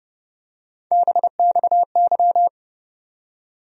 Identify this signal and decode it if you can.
Morse code, heard as a keyed tone, about 32 wpm, 700 Hz. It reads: BXY
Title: USA Call Sign Postfixes - 3 Letters 30wpm